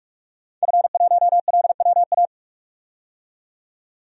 Morse code, heard as a keyed tone, about 44 wpm, 700 Hz. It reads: F1LWA
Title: DXCC-100 Call Signs - 5 Characters - 45wpm